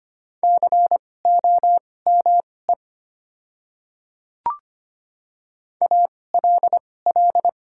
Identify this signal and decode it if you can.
Morse code, heard as a keyed tone, about 25 wpm, spaced as for 18 wpm, 700 Hz. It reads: COME ALL
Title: 2x Character Spacing - Top 200 Words - 25wpm